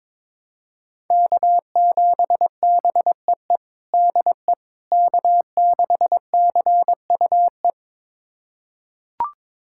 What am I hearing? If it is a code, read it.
Morse code, heard as a keyed tone, about 22 wpm, 700 Hz. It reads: K7BEE DE K6CUE
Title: USA Callsign DE Callsign - 5 Characters - 22wpm